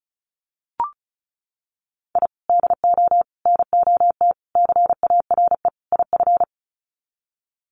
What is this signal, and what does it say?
Morse code, heard as a keyed tone, about 34 wpm, 700 Hz. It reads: I DO NOT CARE IF
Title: Sets of 5 Words 35wpm